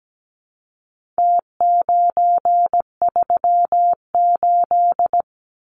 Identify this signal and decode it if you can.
Morse code, heard as a keyed tone, about 17 wpm, 700 Hz. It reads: T938
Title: Numbers - Contest Style - 4-Digit 17wpm